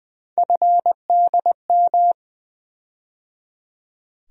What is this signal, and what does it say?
Morse code, heard as a keyed tone, about 20 wpm, 700 Hz. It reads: FDM
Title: USA Call Sign Postfixes - 3 Letters 20wpm